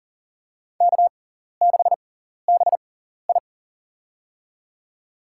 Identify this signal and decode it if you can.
Morse code, heard as a keyed tone, about 40 wpm, spaced as for 14 wpm, 700 Hz. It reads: K6BI
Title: USA Call Signs - 4 Characters - 6x Character Spacing - 40wpm